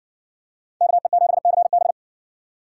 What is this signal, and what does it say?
Morse code, heard as a keyed tone, about 60 wpm, 700 Hz. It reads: KE7CB